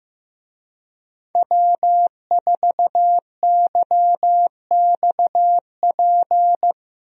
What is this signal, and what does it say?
Morse code, heard as a keyed tone, about 15 wpm, 700 Hz. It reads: W4YXP
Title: USA Call Signs - 5 Characters - Repeated 3x - 15wpm